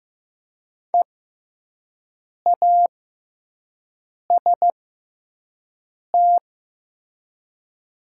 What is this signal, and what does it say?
Morse code, heard as a keyed tone, about 15 wpm, spaced as for 5 wpm, 700 Hz. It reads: EAST
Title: 6x Character Spacing - Top 500 Words - 15wpm